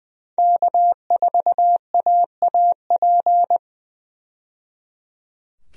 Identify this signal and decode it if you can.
Morse code, heard as a keyed tone, about 20 wpm, 700 Hz. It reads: K4AAP
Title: USA Call Signs - 5 Characters - 20wpm